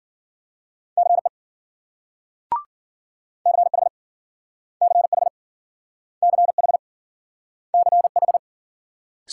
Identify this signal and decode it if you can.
Morse code, heard as a keyed tone, about 53 wpm, 700 Hz. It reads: CE CH CH CH CH